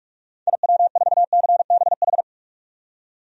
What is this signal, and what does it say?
Morse code, heard as a keyed tone, about 42 wpm, 700 Hz. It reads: IW4CBH
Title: DXCC-100 Call Signs - 6 Characters - 45wpm